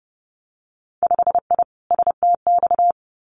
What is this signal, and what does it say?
Morse code, heard as a keyed tone, about 30 wpm, 700 Hz. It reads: 5I STX
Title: Winter Field Day Exchanges Encore 30wpm